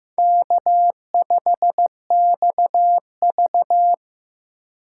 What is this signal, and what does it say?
Morse code, heard as a keyed tone, about 15 wpm, 700 Hz. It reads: K5XV
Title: USA Call Signs - 4 Characters - 2x Character Spacing - 15wpm